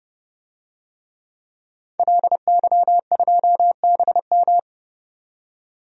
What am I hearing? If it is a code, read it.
Morse code, heard as a keyed tone, about 30 wpm, 700 Hz. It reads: LY2BM
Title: DXCC-100 Call Signs - 5 Characters - 30wpm